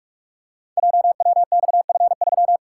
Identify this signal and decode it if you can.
Morse code, heard as a keyed tone, about 44 wpm, 690 Hz. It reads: JWXF3
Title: Code Groups - 5 Letter:Numbers - 45wpm